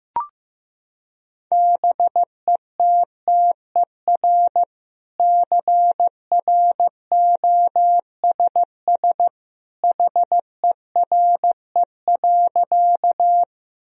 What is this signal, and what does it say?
Morse code, heard as a keyed tone, about 15 wpm, 700 Hz. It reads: BETTER CROSS HERE.
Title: Sentences from Top 1000 Words - Review Part 1 - 15wpm